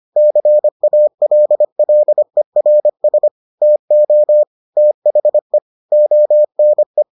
Morse code, heard as a keyed tone, about 25 wpm, 600 Hz. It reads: CALLERS TO THE ONE